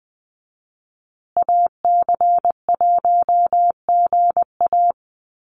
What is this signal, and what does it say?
Morse code, heard as a keyed tone, about 20 wpm, 700 Hz. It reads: AC1GA
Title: USA Call Signs Repeated 2x 20wpm